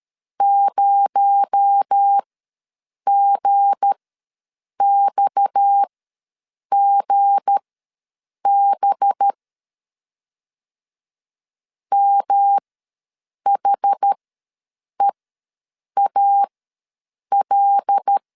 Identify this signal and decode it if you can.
Morse code, heard as a keyed tone, about 13 wpm, spaced as for 7 wpm, 800 Hz. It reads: 0GXGB MHEAL
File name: LEZIONI / 7 GF9Z / 7 G ws 12wpm 3140